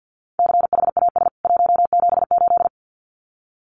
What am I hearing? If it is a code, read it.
Morse code, heard as a keyed tone, about 48 wpm, 700 Hz. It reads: CHAS 178
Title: CWT Exchange 50wpm